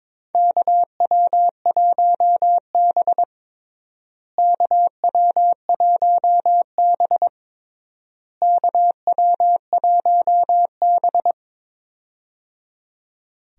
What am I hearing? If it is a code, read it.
Morse code, heard as a keyed tone, about 22 wpm, 700 Hz. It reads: KW1B KW1B KW1B
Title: USA Call Signs - 4 Characters - Repeated 3x - 22wpm